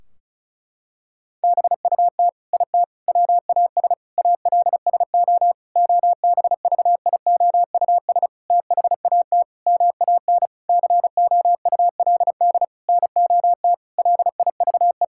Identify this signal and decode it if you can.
Morse code, heard as a keyed tone, about 35 wpm, 700 Hz. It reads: BUT IT WAS ALSO OBVIOUS THAT MAN COULD NOT LIVE
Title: Sentences from Top 1600 Words 35wpm